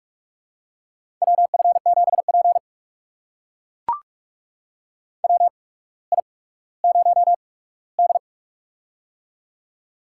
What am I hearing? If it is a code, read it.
Morse code, heard as a keyed tone, about 44 wpm, spaced as for 19 wpm, 700 Hz. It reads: WF7P WI0D
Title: USA Call Signs - 4 Characters - 8x Character Spacing - 45wpm